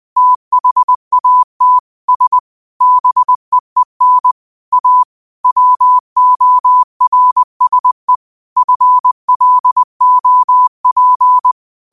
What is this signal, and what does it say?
Morse code, heard as a keyed tone, about 20 wpm, 1000 Hz. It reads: THAT S BEEN A WORSE FLOP
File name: qotdfiles / 20wpm